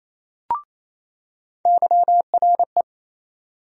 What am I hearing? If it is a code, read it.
Morse code, heard as a keyed tone, about 28 wpm, 700 Hz. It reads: YRE
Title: USA Call Sign Postfixes - 3 Letters 28wpm